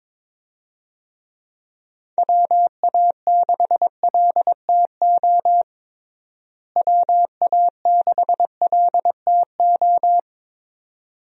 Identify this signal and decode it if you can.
Morse code, heard as a keyed tone, about 22 wpm, 700 Hz. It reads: WA6LTO WA6LTO
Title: USA Call Signs - 6 Characters - Repeated 2x - 22wpm